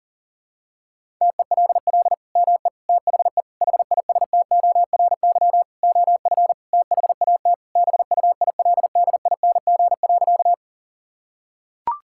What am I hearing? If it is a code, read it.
Morse code, heard as a keyed tone, about 40 wpm, 700 Hz. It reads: TELL ME THE HISTORY OF THAT BUILDING.
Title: Sentences from Top 1000 Words - Review Part 5 - 40wpm